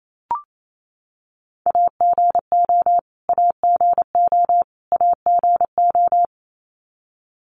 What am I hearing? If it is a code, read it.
Morse code, heard as a keyed tone, about 28 wpm, 700 Hz. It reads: AGO AGO AGO
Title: Top 500 Words Repeated 28wpm